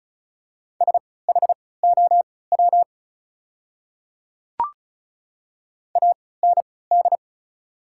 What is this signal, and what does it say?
Morse code, heard as a keyed tone, about 34 wpm, spaced as for 20 wpm, 700 Hz. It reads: SHOW AND
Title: 3x Character Spacing - Top 200 Words - 35wpm